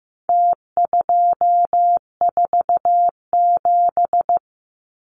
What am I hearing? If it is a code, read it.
Morse code, heard as a keyed tone, about 15 wpm, 700 Hz. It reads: T247
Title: Numbers - Contest Style - 4-Digit 15wpm